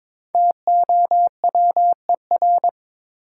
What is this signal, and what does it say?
Morse code, heard as a keyed tone, about 22 wpm, 700 Hz. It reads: TOWER
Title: Five Letter Words - 22wpm